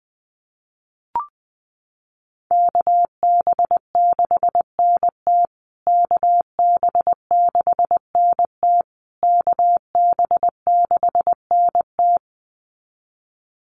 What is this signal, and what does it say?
Morse code, heard as a keyed tone, about 19 wpm, 700 Hz. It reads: KB6NT KB6NT KB6NT
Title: USA Call Signs Repeated 3x 20wpm